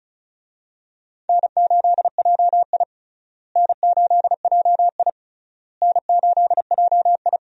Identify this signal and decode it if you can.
Morse code, heard as a keyed tone, about 36 wpm, 700 Hz. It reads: N8JI N8JI N8JI
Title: USA Call Signs - 4 Characters - Repeated 3x - 35wpm